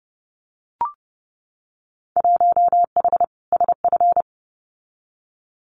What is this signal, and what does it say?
Morse code, heard as a keyed tone, about 30 wpm, 700 Hz. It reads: 1H SF
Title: Winter Field Day Exchanges 30wpm